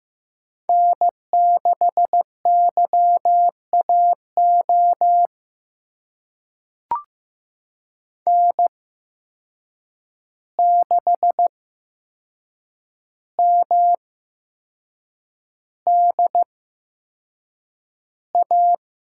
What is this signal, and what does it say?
Morse code, heard as a keyed tone, about 16 wpm, spaced as for 6 wpm, 700 Hz. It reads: N6YAO N6MDA
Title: USA Call Signs - 5 Characters - 8x Character Spacing - 15wpm